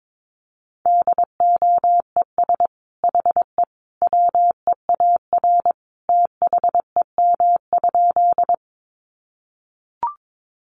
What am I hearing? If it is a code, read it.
Morse code, heard as a keyed tone, about 22 wpm, 700 Hz. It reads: DOES HE WEAR THEM?
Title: Sentences from Top 1500 Words 22wpm